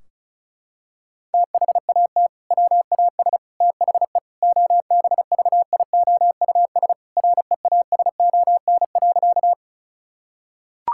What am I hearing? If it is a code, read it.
Morse code, heard as a keyed tone, about 34 wpm, 700 Hz. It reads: THAT WAS THE OBVIOUS REASON.
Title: Sentences from Top 1600 Words 35wpm